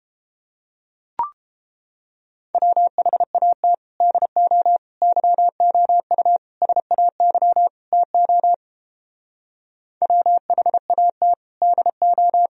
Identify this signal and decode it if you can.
Morse code, heard as a keyed tone, about 29 wpm, 700 Hz. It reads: WHAT DO YOU SAY TO WHAT DO